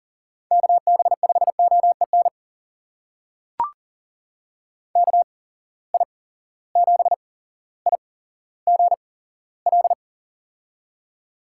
Morse code, heard as a keyed tone, about 40 wpm, spaced as for 17 wpm, 700 Hz. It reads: KB5OEN KI7IGL
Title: USA Call Signs - 6 Characters - 8x Character Spacing - 40wpm